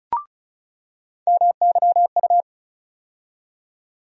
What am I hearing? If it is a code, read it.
Morse code, heard as a keyed tone, about 34 wpm, 700 Hz. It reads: MYU
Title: USA Call Sign Postfixes - 3 Letters 35wpm